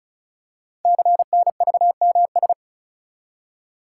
Morse code, heard as a keyed tone, about 36 wpm, 700 Hz. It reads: CNVMS
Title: Code Groups - 5 Letters - 35wpm